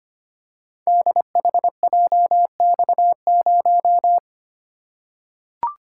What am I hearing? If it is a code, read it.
Morse code, heard as a keyed tone, about 25 wpm, 700 Hz. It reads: DHJX0
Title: Code Groups - 5 Letter:Numbers - 25wpm